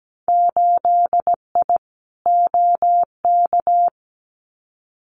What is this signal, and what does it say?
Morse code, heard as a keyed tone, about 17 wpm, 700 Hz. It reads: 8I OK
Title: Winter Field Day Exchanges Encore 17wpm